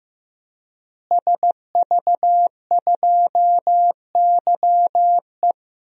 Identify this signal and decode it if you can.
Morse code, heard as a keyed tone, about 16 wpm, 700 Hz. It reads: SV2YE
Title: DXCC-100 Call Signs - 5 Characters - Repeated 3x - 15wpm